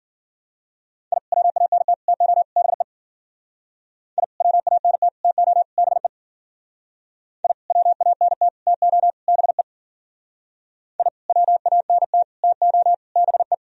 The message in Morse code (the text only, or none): I WANT TO BE I WANT TO BE I WANT TO BE I WANT TO BE